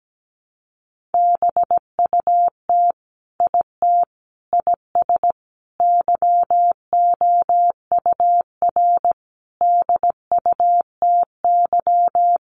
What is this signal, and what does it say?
Morse code, heard as a keyed tone, about 17 wpm, 700 Hz. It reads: BUT IT IS YOUR DUTY
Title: Sentences from Top 1500 Words 17wpm